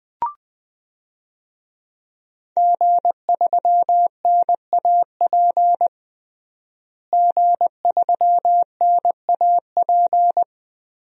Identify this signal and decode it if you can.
Morse code, heard as a keyed tone, about 20 wpm, 700 Hz. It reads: G3NAP G3NAP
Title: DXCC-100 Call Signs - 5 Characters - Repeated 3x - 20wpm